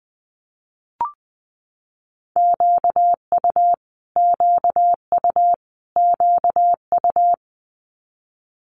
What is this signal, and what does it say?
Morse code, heard as a keyed tone, about 19 wpm, 700 Hz. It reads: QU QU QU
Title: Multi-letter Phonemes Repeated 3x 20wpm